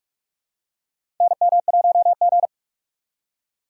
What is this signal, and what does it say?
Morse code, heard as a keyed tone, about 46 wpm, 700 Hz. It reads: NM1G
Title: USA Call Signs - 4 Characters - 3x Character Spacing - 45wpm